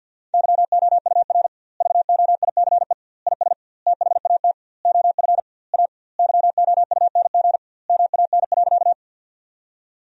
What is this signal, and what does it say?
Morse code, heard as a keyed tone, about 52 wpm, 700 Hz. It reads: YOUR VOICE IS THAT OF A YOUNG MAN.
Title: 2x Word-Spacing - Sentences from Top 500 Words - 50wpm